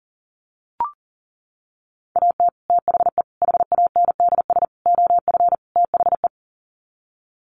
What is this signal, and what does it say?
Morse code, heard as a keyed tone, about 40 wpm, 700 Hz. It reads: AT THE HANDS OF THE
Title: Sets of 5 Words Encore2 40wpm